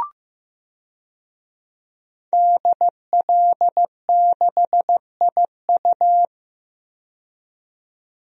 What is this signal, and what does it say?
Morse code, heard as a keyed tone, about 15 wpm, 700 Hz. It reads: DL6IU
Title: DXCC-100 Call Signs - 5 Characters - 15wpm